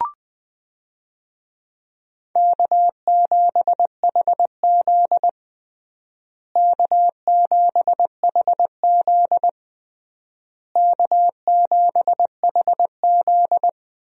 Morse code, heard as a keyed tone, about 20 wpm, 700 Hz. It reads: K7HZ K7HZ K7HZ
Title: USA Call Signs - 4 Characters - Repeated 3x - 20wpm